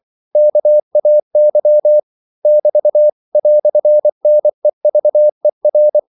Morse code, heard as a keyed tone, about 24 wpm, 600 Hz. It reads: KAY <BT> "NEVER
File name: CW / Practice / EnglishQuotations0424